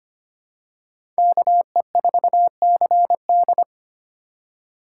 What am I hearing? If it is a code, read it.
Morse code, heard as a keyed tone, about 25 wpm, 700 Hz. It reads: KE4CD
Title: USA Call Signs - 5 Characters - 2x Character Spacing - 25wpm